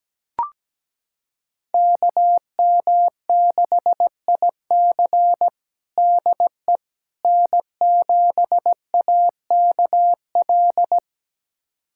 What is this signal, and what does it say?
Morse code, heard as a keyed tone, about 17 wpm, 700 Hz. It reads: KM6IC DE N7AKL
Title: USA Callsign DE Callsign - 5 Characters - 17wpm